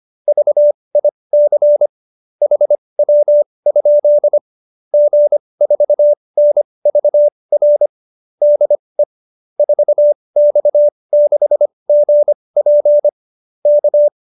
Morse code, heard as a keyed tone, about 25 wpm, spaced as for 20 wpm, 590 Hz. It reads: VIC HW? G4NVR DE 4X6GP K